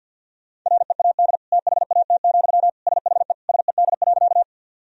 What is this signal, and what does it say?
Morse code, heard as a keyed tone, about 50 wpm, 700 Hz. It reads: READ THAT, SHE SED.
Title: Sentences from Top 200 Words and Abbreviations 50wpm